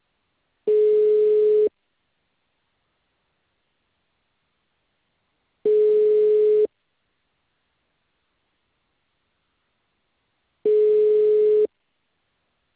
Ringback tone (Europe, 425 Hz).
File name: out- -2003- - - .1649